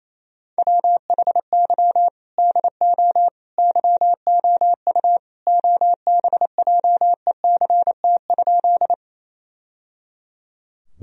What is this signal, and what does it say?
Morse code, heard as a keyed tone, about 28 wpm, 700 Hz. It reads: WHY DO YOU OBJECT?
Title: Sentences from Top 1000 Words - Review Part 3 - 28wpm